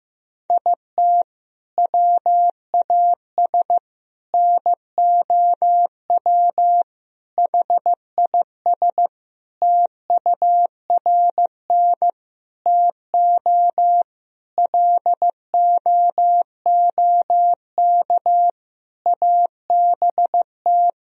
Morse code, heard as a keyed tone, about 15 wpm, 700 Hz. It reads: IT WAS NOW HIS TURN TO LOOK ABT
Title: Sentences from Top 200 Words and Abbreviations 15wpm